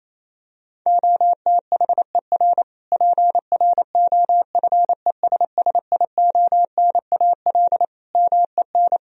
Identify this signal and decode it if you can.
Morse code, heard as a keyed tone, about 28 wpm, 700 Hz. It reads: OTHER PROFESSIONAL MEN
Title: Sentences from Top 1000 Words - Review Part 4 - 28wpm